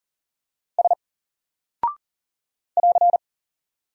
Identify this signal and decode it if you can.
Morse code, heard as a keyed tone, about 39 wpm, 700 Hz. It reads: S <AR>
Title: Single Character (Letter:Number:Punctuation:Prosign) 40wpm